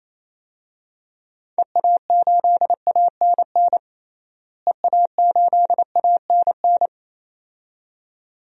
Morse code, heard as a keyed tone, about 28 wpm, 700 Hz. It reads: EA8ANN EA8ANN
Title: DXCC-100 Call Signs - 6 Characters - Repeated 2x - 28wpm